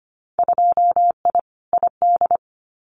Morse code, heard as a keyed tone, about 25 wpm, 700 Hz. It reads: 2I ID
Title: Winter Field Day Exchanges Encore 25wpm